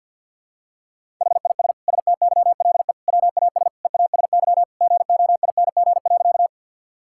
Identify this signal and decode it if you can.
Morse code, heard as a keyed tone, about 49 wpm, 700 Hz. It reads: HIS STYLE WAS EASY GOING.
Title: Sentences from Top 1000 Words - Review Part 5 - 50wpm